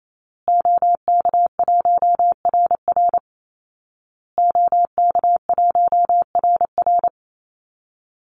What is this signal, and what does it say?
Morse code, heard as a keyed tone, about 28 wpm, 700 Hz. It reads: OK1RR OK1RR
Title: Call Signs Repeated 2x - 28wpm